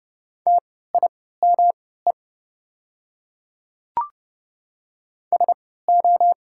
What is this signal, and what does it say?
Morse code, heard as a keyed tone, about 30 wpm, spaced as for 17 wpm, 700 Hz. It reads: TIME SO
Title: 3x Character Spacing - Top 100 Words - 30wpm